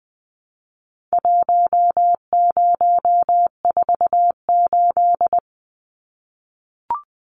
Morse code, heard as a keyed tone, about 20 wpm, 700 Hz. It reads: 1048
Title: Numbers - Contest Style - 4-Digit 20wpm